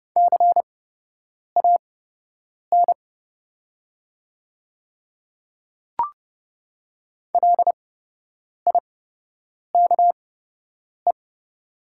Morse code, heard as a keyed tone, about 30 wpm, spaced as for 8 wpm, 700 Hz. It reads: CAN LIKE
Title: 8x Character Spacing - Top 100 Words - 30wpm